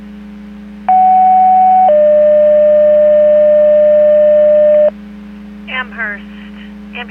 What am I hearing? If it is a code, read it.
Two-tone page, heard as a one-second tone, then a three-second tone, about 750 then 600 Hz, with speech. It is heard from 0.9 to 4.9 s.